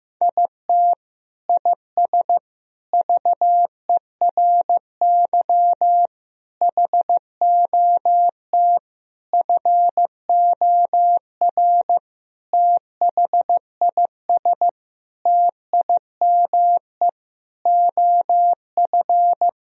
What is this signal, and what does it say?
Morse code, heard as a keyed tone, about 15 wpm, 700 Hz. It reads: IT IS VERY HOT FOR THIS TIME OF
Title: Sentences from Top 1000 Words - Review Part 3 - 15wpm